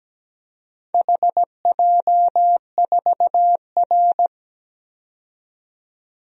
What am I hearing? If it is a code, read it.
Morse code, heard as a keyed tone, about 17 wpm, 700 Hz. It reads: HJ4R
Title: DXCC-100 Call Signs - 4 Characters - 17wpm